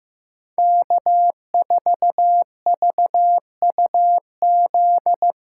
Morse code, heard as a keyed tone, about 15 wpm, 700 Hz. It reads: K4VUZ
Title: USA Callsign DE Callsign 15wpm